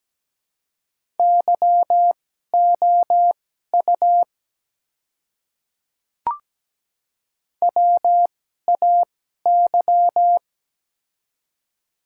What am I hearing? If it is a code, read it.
Morse code, heard as a keyed tone, about 17 wpm, spaced as for 12 wpm, 700 Hz. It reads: YOU WAY
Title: 2x Character Spacing - Top 100 Words - 17wpm